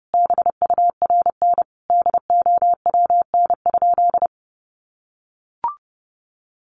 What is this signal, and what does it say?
Morse code, heard as a keyed tone, about 30 wpm, 700 Hz. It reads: BURN DOWN?
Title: Sentences from Top 1400 Words 30wpm